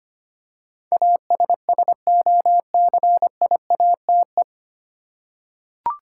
Morse code, heard as a keyed tone, about 25 wpm, 700 Hz. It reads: ASSOCIATE
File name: practice / 25wpm / 9 Letter Words - First 3 Letters Given - 25wpm